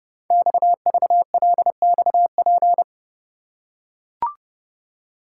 Morse code, heard as a keyed tone, about 30 wpm, 700 Hz. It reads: XVLXP
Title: Code Groups - 5 Letters - 30wpm